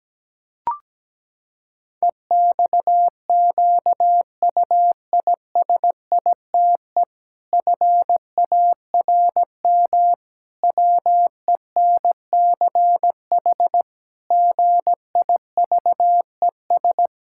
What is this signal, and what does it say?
Morse code, heard as a keyed tone, about 17 wpm, 700 Hz. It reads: EXQUISITE FARM WENCH GIVES
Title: Pangram Challenge v2 - 17wpm